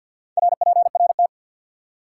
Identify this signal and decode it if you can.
Morse code, heard as a keyed tone, about 54 wpm, 700 Hz. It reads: RPRT
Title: Top 90 QSO Elements 50wpm